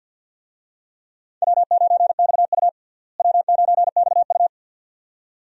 Morse code, heard as a keyed tone, about 50 wpm, 700 Hz. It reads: W9XU W9XU
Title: USA Call Signs - 4 Characters - Repeated 2x - 50wpm